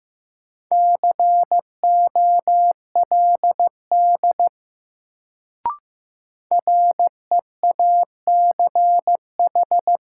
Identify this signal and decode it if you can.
Morse code, heard as a keyed tone, about 15 wpm, 700 Hz. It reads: COLD REACH
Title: Sentences from Top 400 Words - New Word Drill - 15wpm